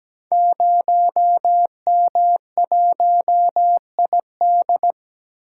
Morse code, heard as a keyed tone, about 17 wpm, 700 Hz. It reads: 0M1ID
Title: Code Groups - 5 Letter:Numbers - 17wpm